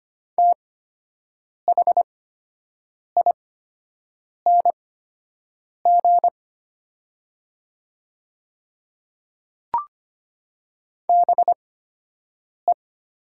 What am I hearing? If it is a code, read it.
Morse code, heard as a keyed tone, about 25 wpm, spaced as for 7 wpm, 700 Hz. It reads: THING BE
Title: 8x Character Spacing - Top 100 Words - 25wpm